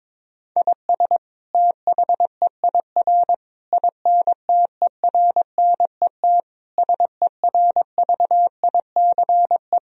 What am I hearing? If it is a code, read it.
Morse code, heard as a keyed tone, about 22 wpm, 700 Hz. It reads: IS THEIR INTERNET SERVICE
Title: Sentences from Top 1000 Words - Review Part 4 - 22wpm